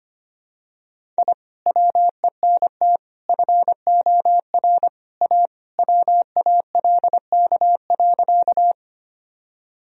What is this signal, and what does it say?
Morse code, heard as a keyed tone, about 25 wpm, 700 Hz. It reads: I WENT FOR A WALK.
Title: Sentences from Top 1000 Words - Review Part 1 - 25wpm